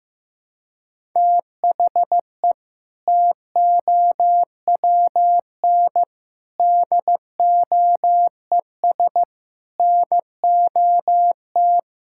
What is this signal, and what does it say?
Morse code, heard as a keyed tone, about 15 wpm, 700 Hz. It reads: THE TOWN DOES NOT
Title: Sentences from Top 1000 Words - Review Part 5 - 15wpm